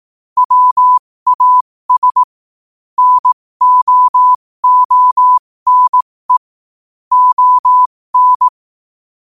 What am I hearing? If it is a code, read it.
Morse code, heard as a keyed tone, about 18 wpm, spaced as for 15 wpm, 1000 Hz. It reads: WAS NO ONE ON
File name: qotdfiles / 15wpm